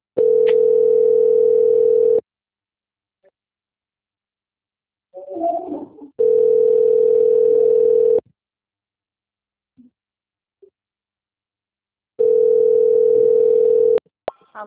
Ringback tone (North America), 440 + 480 Hz.
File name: out- - - - - .12959